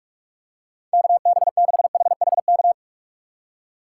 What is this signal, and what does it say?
Morse code, heard as a keyed tone, about 46 wpm, 700 Hz. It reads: KB6HHK
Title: USA Call Signs - 6 Characters - 4x Character Spacing - 45wpm